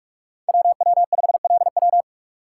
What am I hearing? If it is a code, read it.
Morse code, heard as a keyed tone, about 46 wpm, 700 Hz. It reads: WW5LW
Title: USA Call Signs - 5 Characters - 4x Character Spacing - 45wpm